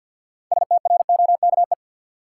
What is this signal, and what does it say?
Morse code, heard as a keyed tone, about 53 wpm, 700 Hz. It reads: STROKE
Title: Six Letter Words - 50wpm